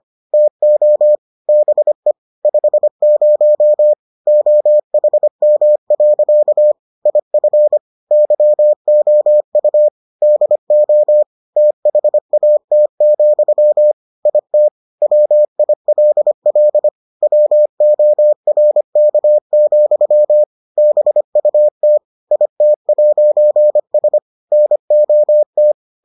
Morse code, heard as a keyed tone, about 25 wpm, 600 Hz. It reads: TO BE 50 OHM. IF YOU DO THAT, IT WILL WORK, BUT IT'S NOT